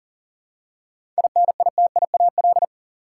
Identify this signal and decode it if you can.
Morse code, heard as a keyed tone, about 40 wpm, 700 Hz. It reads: INITIAL
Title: Seven Letter Words - 40wpm